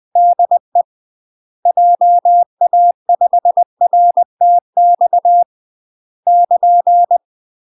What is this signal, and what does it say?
Morse code, heard as a keyed tone, about 20 wpm, 700 Hz. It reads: DE JA5RTX <KN>
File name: UK MASTER QSOs TRAINING / 20WPM / UK MASTER QSOs 0044